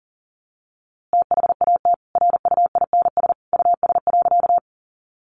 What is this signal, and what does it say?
Morse code, heard as a keyed tone, about 40 wpm, 700 Hz. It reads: THAT RUINS US.
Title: A Sound Of Thunder 40wpm